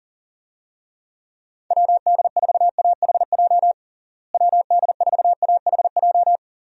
Morse code, heard as a keyed tone, about 39 wpm, 700 Hz. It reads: WD4AHJ WD4AHJ
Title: USA Call Signs - 6 Characters - Repeated 2x - 40wpm